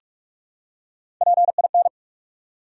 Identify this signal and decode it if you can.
Morse code, heard as a keyed tone, about 43 wpm, 700 Hz. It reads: PIN
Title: Three Letter Words - 45wpm